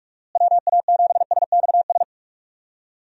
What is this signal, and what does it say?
Morse code, heard as a keyed tone, about 41 wpm, 700 Hz. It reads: WA7SXS
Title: USA Call Signs - 6 Characters - Repeated 3x - 45wpm